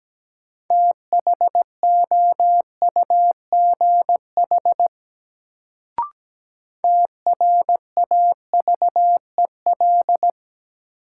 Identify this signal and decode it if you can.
Morse code, heard as a keyed tone, about 17 wpm, 700 Hz. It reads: THOUGH TRAVEL
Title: Sentences from Top 400 Words - New Word Drill - 17wpm